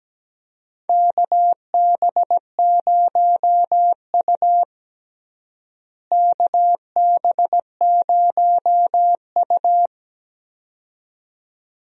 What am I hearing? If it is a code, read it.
Morse code, heard as a keyed tone, about 17 wpm, 700 Hz. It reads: KB0U KB0U
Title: USA Call Signs - 4 Characters - Repeated 2x - 17wpm